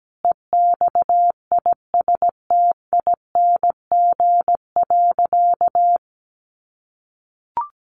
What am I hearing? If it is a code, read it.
Morse code, heard as a keyed tone, about 17 wpm, 700 Hz. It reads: EXISTING.
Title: Sentences from Top 1500 Words 17wpm